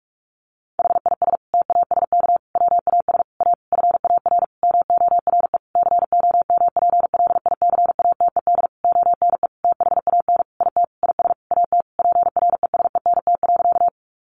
Morse code, heard as a keyed tone, about 44 wpm, 700 Hz. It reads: HIS TASK WAS A FAR MORE COMPLICATED ONE THAN IT IS AT PRESENT.
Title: Sentences from Top 1200 Words 45wpm